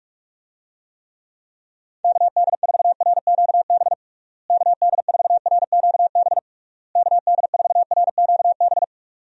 Morse code, heard as a keyed tone, about 46 wpm, 700 Hz. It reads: KD4RQB KD4RQB KD4RQB
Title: USA Call Signs - 6 Characters - Repeated 3x - 45wpm